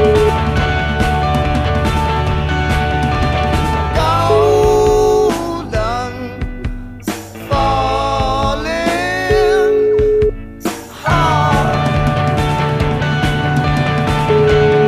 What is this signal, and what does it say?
Ringback tone (Europe, 425 Hz), with noise.